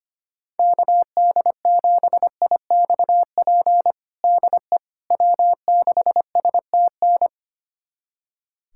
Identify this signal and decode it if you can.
Morse code, heard as a keyed tone, about 25 wpm, 700 Hz. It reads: KD7IXP DE W6STN
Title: USA Callsign DE Callsign 25wpm